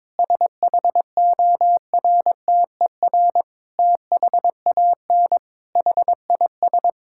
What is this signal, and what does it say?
Morse code, heard as a keyed tone, about 22 wpm, 700 Hz. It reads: SHORTER THAN HIS
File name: Difficult Sentences with Hints - 22wpm